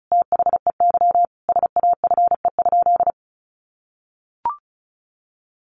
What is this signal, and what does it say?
Morse code, heard as a keyed tone, about 34 wpm, 700 Hz. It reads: THEY SAFE?
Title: Sentences from Top 900 Words 35wpm